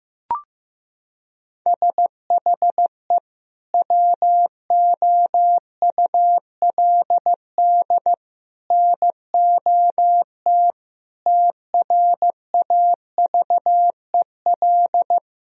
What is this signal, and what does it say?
Morse code, heard as a keyed tone, about 15 wpm, 700 Hz. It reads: SHE WOULD NOT TRAVEL
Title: Sentences from Top 1000 Words - Review Part 2 - 15wpm